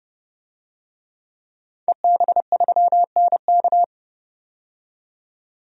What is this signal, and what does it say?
Morse code, heard as a keyed tone, about 31 wpm, 700 Hz. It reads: EB3NK
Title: DXCC-100 Call Signs - 5 Characters - 30wpm